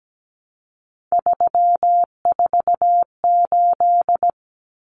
Morse code, heard as a keyed tone, about 17 wpm, 700 Hz. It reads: 348
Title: Common Cut Numbers - 3-Digit 17wpm